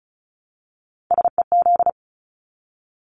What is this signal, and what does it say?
Morse code, heard as a keyed tone, about 34 wpm, 700 Hz. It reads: SEZ
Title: Abbreviations 35wpm